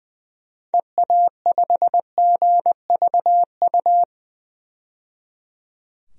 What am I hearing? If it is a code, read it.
Morse code, heard as a keyed tone, about 20 wpm, 700 Hz. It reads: EA5GVU
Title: DXCC-100 Call Signs - 6 Characters - 20wpm